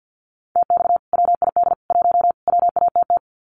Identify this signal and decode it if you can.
Morse code, heard as a keyed tone, about 50 wpm, 700 Hz. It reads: TX WID 1 WATT
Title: Common QSO Phrases 50wpm